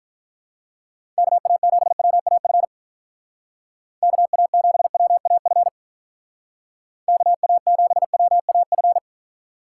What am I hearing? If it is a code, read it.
Morse code, heard as a keyed tone, about 43 wpm, 700 Hz. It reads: KA7WAF KA7WAF KA7WAF
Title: Speed Racing - USA 2x3 Call Signs 35wpm